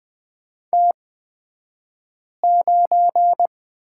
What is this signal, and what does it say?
Morse code, heard as a keyed tone, about 20 wpm, spaced as for 5 wpm, 700 Hz. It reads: T9
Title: 235 Lesson 38d - Letter q - Two Characters 20wpm